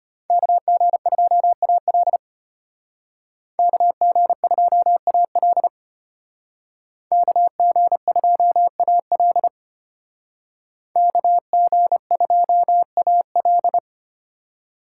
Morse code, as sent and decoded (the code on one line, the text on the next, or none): -.- --. ..--- .- .-.. / -.- --. ..--- .- .-.. / -.- --. ..--- .- .-.. / -.- --. ..--- .- .-..
KG2AL KG2AL KG2AL KG2AL